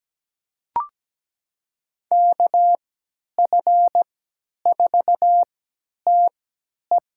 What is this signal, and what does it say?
Morse code, heard as a keyed tone, about 17 wpm, spaced as for 10 wpm, 700 Hz. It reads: KF4TE
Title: USA Call Signs - 5 Characters - 3x Character Spacing - 17wpm